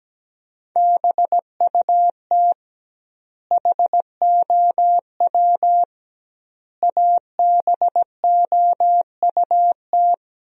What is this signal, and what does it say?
Morse code, heard as a keyed tone, about 17 wpm, 700 Hz. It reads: BUT HOW ABOUT
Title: 2x Word-Spacing - Sentences from Top 100 Words - 17wpm